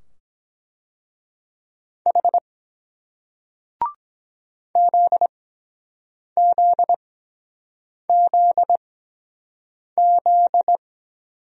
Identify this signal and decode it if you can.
Morse code, heard as a keyed tone, about 23 wpm, spaced as for 7 wpm, 700 Hz. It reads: H ZZZZ